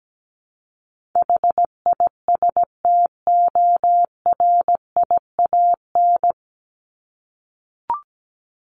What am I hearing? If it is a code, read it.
Morse code, heard as a keyed tone, about 17 wpm, 700 Hz. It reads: HISTORIAN
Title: Top 1751-2000 Words 17wpm